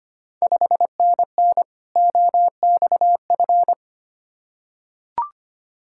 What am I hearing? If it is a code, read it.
Morse code, heard as a keyed tone, about 25 wpm, 700 Hz. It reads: 5NN OXF
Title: Ontario QSO Party Exchange 25wpm